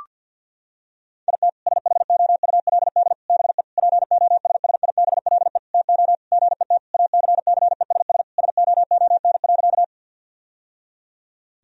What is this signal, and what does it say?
Morse code, heard as a keyed tone, about 50 wpm, 700 Hz. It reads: IT SHOULD BE POSSIBLE TO GET ACCESS SOON.
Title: Sentences from Top 1000 Words - Review Part 4 - 50wpm